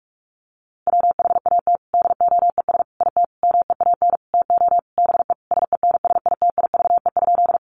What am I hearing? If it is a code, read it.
Morse code, heard as a keyed tone, about 46 wpm, 700 Hz. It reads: WHAT DOES IT MEAN TO BE SENSITIVE?
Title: Sentences from Top 1400 Words 45wpm